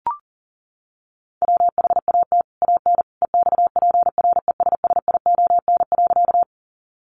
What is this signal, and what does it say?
Morse code, heard as a keyed tone, about 40 wpm, 700 Hz. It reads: WHAT AN EXPRESSION.
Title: Sentences from Top 1200 Words 40wpm